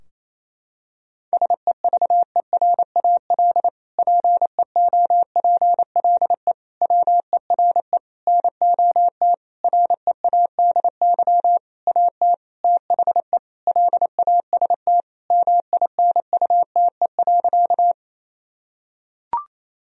Morse code, heard as a keyed tone, about 28 wpm, 700 Hz. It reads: SEVERAL PEOPLE WERE NOT READY AT THE LAST MINUTE.
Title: Sentences from Top 1000 Words - Review Part 2 - 28wpm